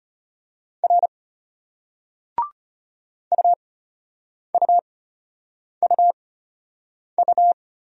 Morse code, heard as a keyed tone, about 33 wpm, spaced as for 8 wpm, 700 Hz. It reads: R UUUU